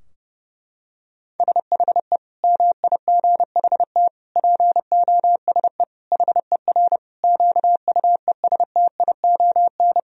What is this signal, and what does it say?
Morse code, heard as a keyed tone, about 30 wpm, 700 Hz. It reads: SHE MIGHT POSE HER QUESTION
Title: Sentences from Top 1000 Words - Review Part 2 - 30wpm